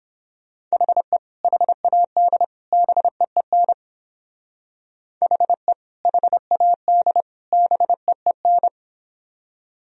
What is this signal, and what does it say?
Morse code, heard as a keyed tone, about 27 wpm, 700 Hz. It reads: HE HAD BEEN HE HAD BEEN
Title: Speed-Racing - Sets of 3 Words 22wpm